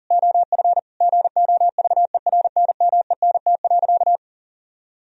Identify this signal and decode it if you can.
Morse code, heard as a keyed tone, about 37 wpm, 700 Hz. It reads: OF GOVERNMENT.
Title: Sentences from Top 1000 Words - Review Part 3 - 40wpm